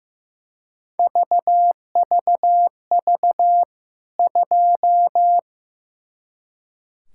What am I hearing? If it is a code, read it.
Morse code, heard as a keyed tone, about 15 wpm, 700 Hz. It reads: VVV 2
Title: K1USN SST QSO Practice 15wpm